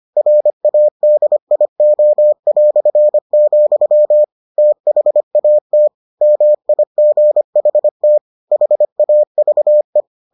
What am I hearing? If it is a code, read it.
Morse code, heard as a keyed tone, about 25 wpm, 600 Hz. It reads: RADIO", THAT MIGHT HAVE